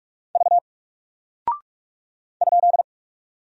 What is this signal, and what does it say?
Morse code, heard as a keyed tone, about 44 wpm, 700 Hz. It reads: V ?
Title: Single Character (Letter:Number:Punctuation:Prosign) 45wpm